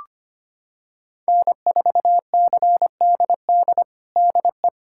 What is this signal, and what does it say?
Morse code, heard as a keyed tone, about 25 wpm, 700 Hz. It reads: N4CDD DE
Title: USA Callsign DE Callsign - 5 Characters - 25wpm